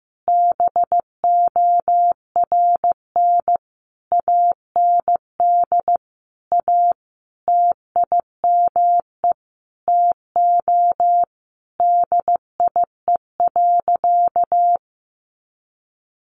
Morse code, heard as a keyed tone, about 15 wpm, 700 Hz. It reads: BORN AND A TIME TO DIE.
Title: Sentences from Top 1300 Words 15wpm